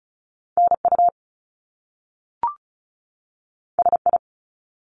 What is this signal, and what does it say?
Morse code, heard as a keyed tone, about 36 wpm, 700 Hz. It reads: NU SI
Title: Top Two Letter Combinations in Words 35wpm